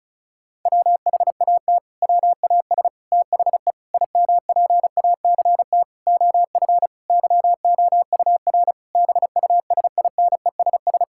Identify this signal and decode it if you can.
Morse code, heard as a keyed tone, about 36 wpm, 700 Hz. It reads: WHAT WAS THE IMPACT OF YOUR BUSINESS
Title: Sentences from Top 1000 Words - Review Part 6 - 35wpm